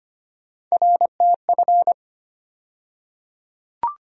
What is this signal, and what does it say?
Morse code, heard as a keyed tone, about 25 wpm, 700 Hz. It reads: RTF
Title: USA Call Sign Postfixes - 3 Letters 25wpm